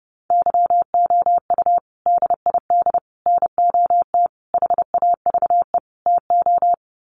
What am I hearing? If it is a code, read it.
Morse code, heard as a keyed tone, about 30 wpm, 700 Hz. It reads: YOU DID NOT HAVE TO
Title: Sets of 5 Words Encore 30wpm